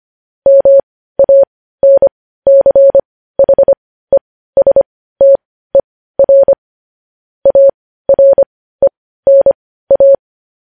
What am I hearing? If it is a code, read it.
Morse code, heard as a keyed tone, about 25 wpm, spaced as for 15 wpm, 550 Hz. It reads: MANCHESTER ARENA
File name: podcast / News headlines-1x-25x15